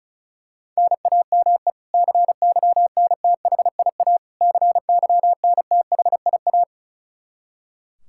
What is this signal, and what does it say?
Morse code, heard as a keyed tone, about 34 wpm, 700 Hz. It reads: NAME CYNTHIA CYNTHIA
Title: K1USN SST QSO Practice 35wpm